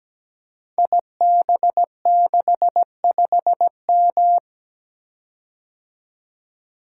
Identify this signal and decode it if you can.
Morse code, heard as a keyed tone, about 17 wpm, 700 Hz. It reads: IB65M
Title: Code Groups - 5 Letter:Numbers - 17wpm